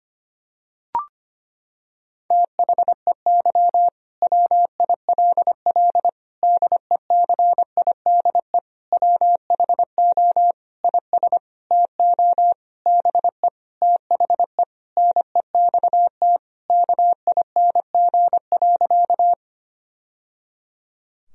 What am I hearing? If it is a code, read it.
Morse code, heard as a keyed tone, about 25 wpm, 700 Hz. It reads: THEY WILL DECIDE WHO IS TO BE THE NEXT KING.
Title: Sentences from Top 1000 Words - Review Part 3 - 25wpm